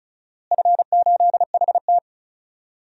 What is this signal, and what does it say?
Morse code, heard as a keyed tone, about 36 wpm, 700 Hz. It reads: F8HT
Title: DXCC-100 Call Signs - 4 Characters - Repeated 3x - 35wpm